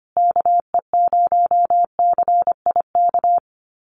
Morse code, heard as a keyed tone, about 25 wpm, 700 Hz. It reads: KE0CIK
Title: USA Call Signs Repeated 3x 25wpm